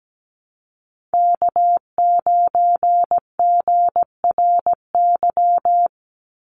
Morse code, heard as a keyed tone, about 17 wpm, 700 Hz. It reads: K9GRY
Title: USA Call Signs 17wpm